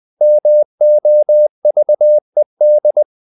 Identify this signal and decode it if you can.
Morse code, heard as a keyed tone, about 20 wpm, 600 Hz. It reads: MOVED
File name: the old man and the sea / the old man and the sea part 1 20wpm